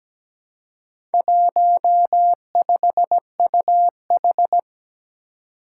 Morse code, heard as a keyed tone, about 17 wpm, 700 Hz. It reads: 15UH
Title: Four Challenging Letter-Number Combinations 17wpm